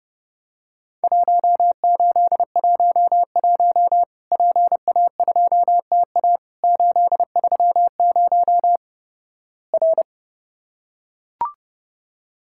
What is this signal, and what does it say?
Morse code, heard as a keyed tone, about 30 wpm, 700 Hz. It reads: 1811 PA2TA 830 R
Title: Worked All Europe DX Contest - QTC Exchange - 30wpm